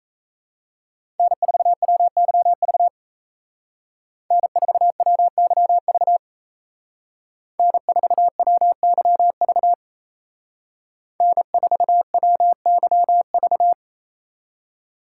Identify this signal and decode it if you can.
Morse code, heard as a keyed tone, about 36 wpm, 700 Hz. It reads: N4WYV N4WYV N4WYV N4WYV